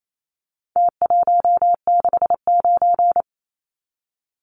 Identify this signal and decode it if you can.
Morse code, heard as a keyed tone, about 28 wpm, 700 Hz. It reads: T169
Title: Numbers - Contest Style - 4-Digit 28wpm